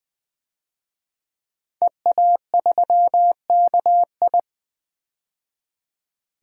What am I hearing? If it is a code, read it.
Morse code, heard as a keyed tone, about 20 wpm, 700 Hz. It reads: EA3KI
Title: DXCC-100 Call Signs - 5 Characters - 20wpm